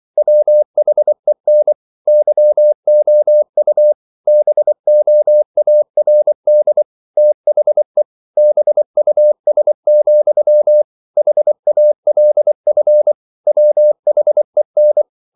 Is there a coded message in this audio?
Yes: Morse code from 0.2 to 15.0 s, heard as a keyed tone, about 24 wpm, 600 Hz. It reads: WHEN YOU BOARD THE BUS, HALF WHEN